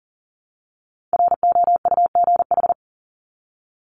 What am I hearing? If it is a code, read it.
Morse code, heard as a keyed tone, about 40 wpm, 700 Hz. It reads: ROUGH
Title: Top 1501-1750 Words 40wpm